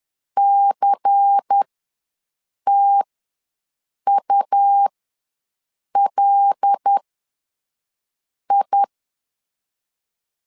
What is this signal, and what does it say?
Morse code, heard as a keyed tone, about 10 wpm, spaced as for 5 wpm, 800 Hz. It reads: CTULI
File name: LEZIONI / 9 2 ws 10wpm 4130